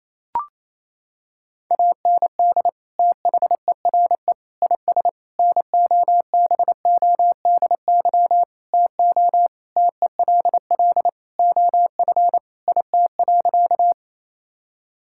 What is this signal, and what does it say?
Morse code, heard as a keyed tone, about 28 wpm, 700 Hz. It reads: AND THERE IS NOBODY TO TELL OF IT.
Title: Sentences from Top 1600 Words 28wpm